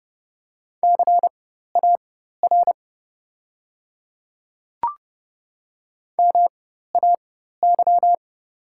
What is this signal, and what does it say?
Morse code, heard as a keyed tone, about 30 wpm, spaced as for 14 wpm, 700 Hz. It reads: CAR MAY
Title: 4x Character Spacing - Top 300 Words - 30wpm